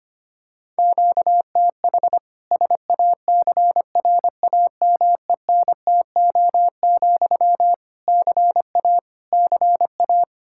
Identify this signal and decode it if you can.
Morse code, heard as a keyed tone, about 25 wpm, 700 Hz. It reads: QTH SACRAMENTO, CA CA
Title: Practice QSO - Exchange 1 - Beginning - 25wpm